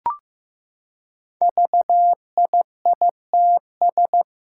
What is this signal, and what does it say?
Morse code, heard as a keyed tone, about 15 wpm, 700 Hz. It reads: VIITS
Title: Five Challenging Letter-Number Combinations 15wpm